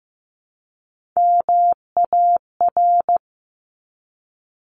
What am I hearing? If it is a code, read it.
Morse code, heard as a keyed tone, about 15 wpm, 700 Hz. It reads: MAR
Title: ARRL Sections 15wpm